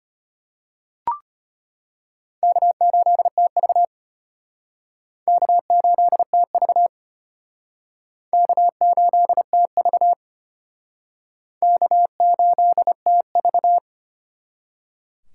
Morse code, heard as a keyed tone, about 34 wpm, 700 Hz. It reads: K8TV K8TV K8TV K8TV